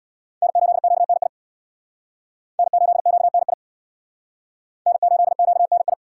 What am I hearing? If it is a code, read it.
Morse code, heard as a keyed tone, about 60 wpm, 700 Hz. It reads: N8QNI N8QNI N8QNI
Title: Speed Racing - USA 1x3 Call Signs 50wpm